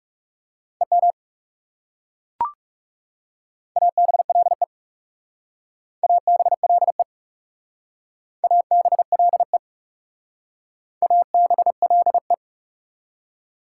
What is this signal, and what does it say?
Morse code, heard as a keyed tone, about 40 wpm, 690 Hz. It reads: EM ABLE ABLE ABLE ABLE